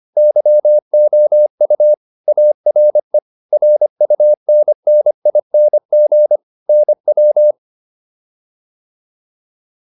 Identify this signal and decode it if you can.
Morse code, heard as a keyed tone, about 25 wpm, 600 Hz. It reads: YOU ARE RUNNING NW
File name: QSO409 25